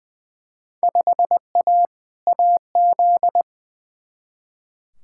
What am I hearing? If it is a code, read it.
Morse code, heard as a keyed tone, about 20 wpm, 700 Hz. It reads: 5A AZ
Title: Field Day Exchanges - 20wpm